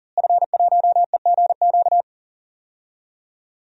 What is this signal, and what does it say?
Morse code, heard as a keyed tone, about 40 wpm, 700 Hz. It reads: F1EGQ
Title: DXCC-100 Call Signs - 5 Characters - Repeated 2x - 40wpm